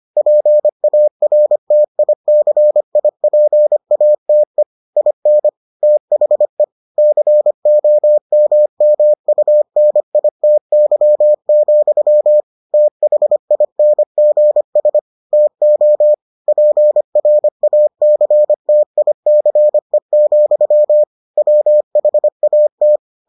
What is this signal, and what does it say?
Morse code, heard as a keyed tone, about 25 wpm, 600 Hz. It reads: PARTICIPATE IN THE COMMUNITY, THINGS TO PRACTICE, WHAT